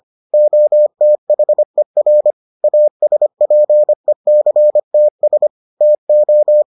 Morse code, heard as a keyed tone, about 25 wpm, 600 Hz. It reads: OTHER ASPECTS TO